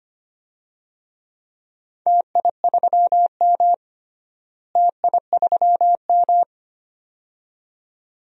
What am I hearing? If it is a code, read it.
Morse code, heard as a keyed tone, about 25 wpm, 700 Hz. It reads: TI3M TI3M
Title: DXCC-100 Call Signs - 4 Characters - Repeated 2x - 25wpm